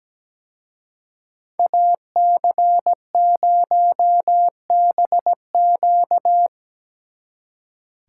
Morse code, heard as a keyed tone, about 19 wpm, 700 Hz. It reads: AC0BQ
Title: Speed Racing - USA Call Signs 17wpm